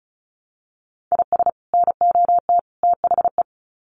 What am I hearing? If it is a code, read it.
Morse code, heard as a keyed tone, about 34 wpm, 700 Hz. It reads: IS NOT THE
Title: Sets of 3 Words 35wpm